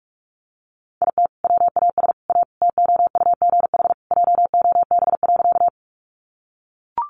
Morse code, heard as a keyed tone, about 44 wpm, 700 Hz. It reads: IT WAS A TOUGH JOB.
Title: Sentences from Top 1200 Words 45wpm